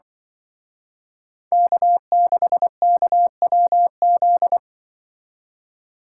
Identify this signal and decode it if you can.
Morse code, heard as a keyed tone, about 25 wpm, 700 Hz. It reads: K6KWZ
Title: Speed Racing - USA 1x3 Call Signs 20wpm